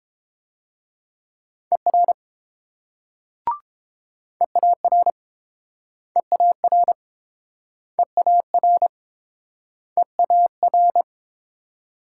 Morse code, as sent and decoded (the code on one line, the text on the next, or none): . .-. / . .- .-. / . .- .-. / . .- .-. / . .- .-.
ER EAR EAR EAR EAR